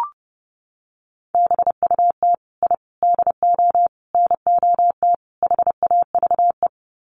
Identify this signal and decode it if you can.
Morse code, heard as a keyed tone, about 30 wpm, 700 Hz. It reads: BUT I DO NOT HAVE
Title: Sets of 5 Words Encore 30wpm